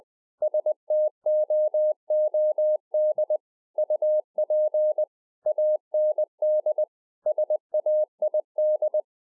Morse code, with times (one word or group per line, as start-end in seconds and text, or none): 0.4-3.4 STOOD
3.8-5.0 UP
5.5-6.8 AND
7.3-9.0 SAID